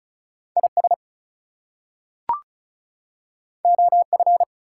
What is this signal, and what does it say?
Morse code, heard as a keyed tone, about 36 wpm, 700 Hz. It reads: IS OF
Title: Two Letter Words - 35wpm